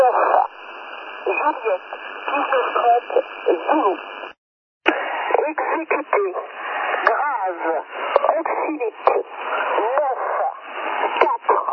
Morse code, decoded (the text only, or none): none